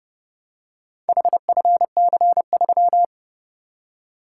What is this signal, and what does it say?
Morse code, heard as a keyed tone, about 29 wpm, 700 Hz. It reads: HFC3